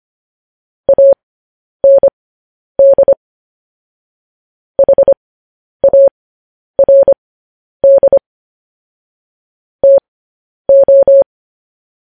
Morse code, heard as a keyed tone, about 25 wpm, spaced as for 10 wpm, 550 Hz. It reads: AND HARD TO
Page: news headlines in morse code at 10 wpm